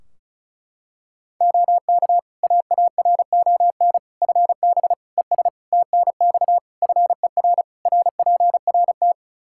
Morse code, heard as a keyed tone, about 36 wpm, 700 Hz. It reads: OK AARON FB ES TNX FER RPRT
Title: K1USN SST QSO Practice 35wpm